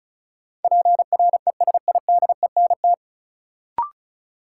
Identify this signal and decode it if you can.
Morse code, heard as a keyed tone, about 36 wpm, 700 Hz. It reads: PRESIDENT
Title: Sets of 2 Words - New Word Drill - 35wpm